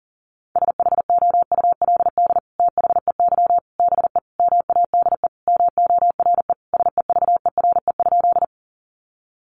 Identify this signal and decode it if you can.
Morse code, heard as a keyed tone, about 40 wpm, 700 Hz. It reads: SHOULD THEY BE MADE MORE SEVERE?
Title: Sentences from Top 1400 Words 40wpm